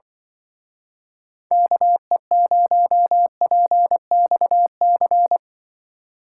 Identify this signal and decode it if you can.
Morse code, heard as a keyed tone, about 25 wpm, 700 Hz. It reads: KE0PXC
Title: Speed Racing - USA 2x3 Call Signs 20wpm